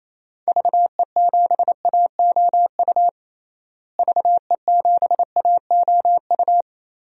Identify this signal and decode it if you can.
Morse code, heard as a keyed tone, about 28 wpm, 700 Hz. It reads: VE7AOU VE7AOU
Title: DXCC-100 Call Signs - 6 Characters - Repeated 2x - 28wpm